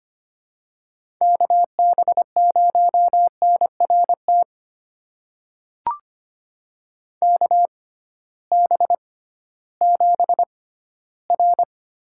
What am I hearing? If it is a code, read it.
Morse code, heard as a keyed tone, about 25 wpm, spaced as for 15 wpm, 700 Hz. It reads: KB0NRT KB7R
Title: USA Call Signs - 6 Characters - 6x Character Spacing - 25wpm